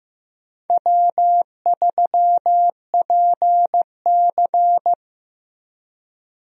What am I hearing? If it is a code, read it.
Morse code, heard as a keyed tone, about 15 wpm, 700 Hz. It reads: W3PC
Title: USA Call Signs - 4 Characters - 15wpm